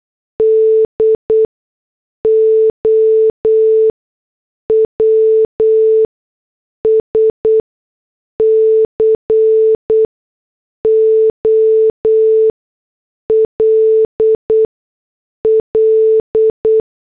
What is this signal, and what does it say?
Morse code, heard as a keyed tone, about 8 wpm, spaced as for 6 wpm, 440 Hz. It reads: DOWS COLL